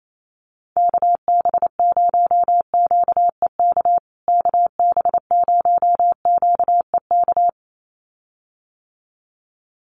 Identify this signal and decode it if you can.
Morse code, heard as a keyed tone, about 28 wpm, 700 Hz. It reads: KB0QEK KB0QEK
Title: USA Call Signs Repeated 2x 28wpm